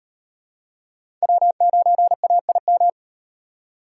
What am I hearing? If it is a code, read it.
Morse code, heard as a keyed tone, about 34 wpm, 700 Hz. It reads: W9AIM